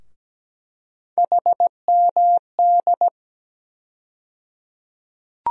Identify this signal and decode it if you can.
Morse code, heard as a keyed tone, about 17 wpm, 700 Hz. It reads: HMD
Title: USA Call Sign Postfixes - 3 Letters 17wpm